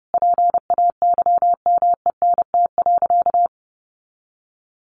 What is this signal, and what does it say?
Morse code, heard as a keyed tone, about 30 wpm, 700 Hz. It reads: PAYMENT.
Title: Sentences from Top 1400 Words 30wpm